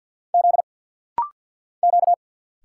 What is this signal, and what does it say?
Morse code, heard as a keyed tone, about 50 wpm, 700 Hz. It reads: Z Q
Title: Single Letter-Number - Rapid-Fire 50wpm